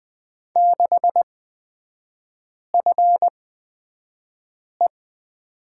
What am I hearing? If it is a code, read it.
Morse code, heard as a keyed tone, about 20 wpm, spaced as for 5 wpm, 700 Hz. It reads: 6FE
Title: 250 Lesson 40e - BK - Three Characters 20wpm